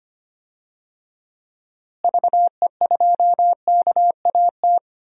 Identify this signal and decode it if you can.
Morse code, heard as a keyed tone, about 25 wpm, 700 Hz. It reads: VE2KAT
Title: DXCC-100 Call Signs - 6 Characters - Repeated 2x - 25wpm